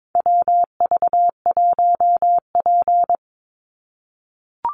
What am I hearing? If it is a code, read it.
Morse code, heard as a keyed tone, about 22 wpm, 700 Hz. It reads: WV1P
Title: USA Call Signs 22wpm